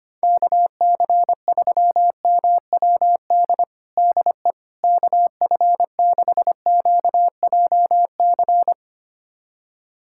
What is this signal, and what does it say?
Morse code, heard as a keyed tone, about 25 wpm, 700 Hz. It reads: KC3MWD DE KF6QJC
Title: USA Callsign DE Callsign 25wpm